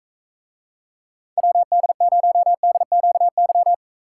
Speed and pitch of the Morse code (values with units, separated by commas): 44 wpm, 700 Hz